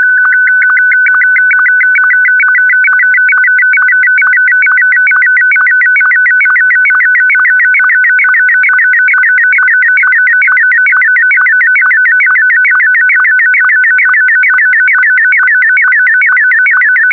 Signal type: SSTV (slow-scan television)